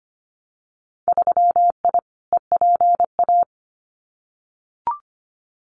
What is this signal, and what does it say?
Morse code, heard as a keyed tone, about 25 wpm, 700 Hz. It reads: 3I EPA
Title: Winter Field Day Exchanges Encore 25wpm